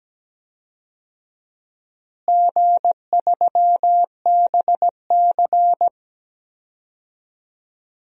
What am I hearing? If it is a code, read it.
Morse code, heard as a keyed tone, about 17 wpm, 700 Hz. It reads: G3BC
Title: DXCC-100 Call Signs - 4 Characters - Repeated 3x - 17wpm